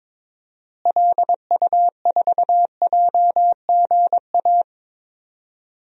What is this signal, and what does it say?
Morse code, heard as a keyed tone, about 21 wpm, 700 Hz. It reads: LU4JGA
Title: DXCC-100 Call Signs - 6 Characters - Repeated 3x - 22wpm